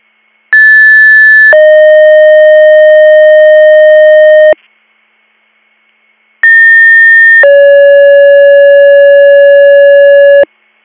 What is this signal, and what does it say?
Two-tone page, heard as a one-second tone, then a three-second tone, about 1750 then 620 Hz, about 1800 then 560 Hz. It is heard from 0.5 to 10.4 s.